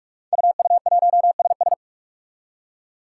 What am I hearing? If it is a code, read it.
Morse code, heard as a keyed tone, about 42 wpm, 700 Hz. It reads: UU1SS
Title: Five Challenging Letter-Number Combinations 45wpm